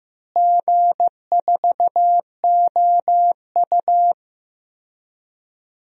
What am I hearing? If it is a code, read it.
Morse code, heard as a keyed tone, about 16 wpm, 700 Hz. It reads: G4OU
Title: DXCC-100 Call Signs - 4 Characters - Repeated 2x - 15wpm